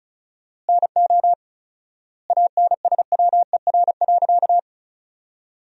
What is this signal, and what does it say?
Morse code, heard as a keyed tone, about 34 wpm, 700 Hz. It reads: NO ANSWER.
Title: 4x Word-Spacing - Sentences from Top 300 Words - 35wpm